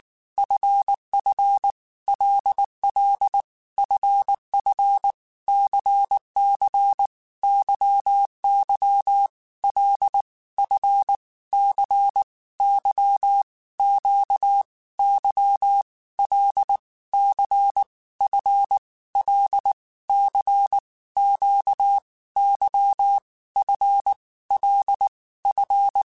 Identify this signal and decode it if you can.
Morse code, heard as a keyed tone, about 19 wpm, 800 Hz. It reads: FF LL FF CC YY L F C Y Q Y L C F L C Q Y F L F
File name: Groups@18 / Groups@18 mp3 / LFCYQ@18